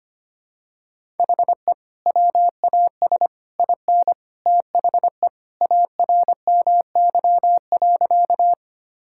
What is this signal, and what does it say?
Morse code, heard as a keyed tone, about 25 wpm, 700 Hz. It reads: HE WAS IN THE ARMY.
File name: practice / 25wpm / Sentences with 3-Word Phrases - 25wpm